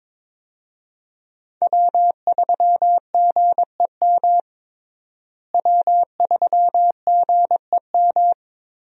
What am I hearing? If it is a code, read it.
Morse code, heard as a keyed tone, about 22 wpm, 700 Hz. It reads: W3GEM W3GEM
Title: USA Call Signs - 5 Characters - Repeated 2x - 22wpm